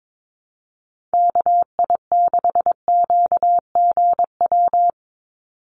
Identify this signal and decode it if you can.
Morse code, heard as a keyed tone, about 22 wpm, 700 Hz. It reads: KI6QGW
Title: USA Call Signs 22wpm